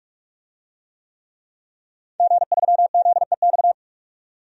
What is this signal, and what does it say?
Morse code, heard as a keyed tone, about 46 wpm, 700 Hz. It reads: G3ZEX
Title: DXCC-100 Call Signs - 5 Characters - 45wpm